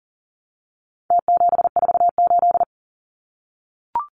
Morse code, heard as a keyed tone, about 40 wpm, 700 Hz. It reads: T748
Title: Numbers - Contest Style - 4-Digit 40wpm